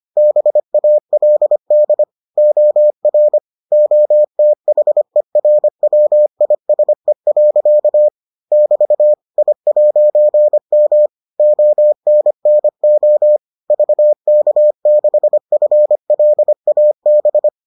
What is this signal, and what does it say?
Morse code, heard as a keyed tone, about 25 wpm, 600 Hz. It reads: BALD OR OTHERWISE. <BT> I'M ONNO VK6FLAB